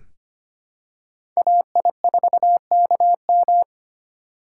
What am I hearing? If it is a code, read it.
Morse code, heard as a keyed tone, about 25 wpm, 700 Hz. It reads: AI4KM
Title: USA Call Signs - 5 Characters - 4x Character Spacing - 25wpm